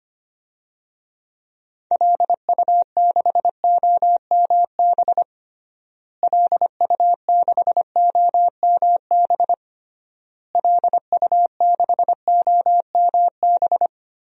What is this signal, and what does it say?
Morse code, heard as a keyed tone, about 25 wpm, 700 Hz. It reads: LU6OMB LU6OMB LU6OMB
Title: DXCC-100 Call Signs - 6 Characters - Repeated 3x - 25wpm